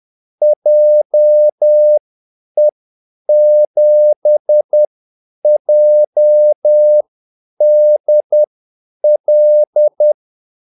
Morse code, heard as a keyed tone, about 10 wpm, spaced as for 8 wpm, 600 Hz. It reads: JE7JDL
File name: Callsigns-10-8wpm